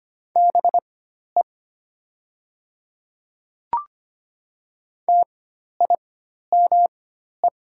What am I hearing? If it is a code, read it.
Morse code, heard as a keyed tone, about 25 wpm, spaced as for 12 wpm, 700 Hz. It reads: BE TIME
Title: 4x Character Spacing - Top 300 Words - 25wpm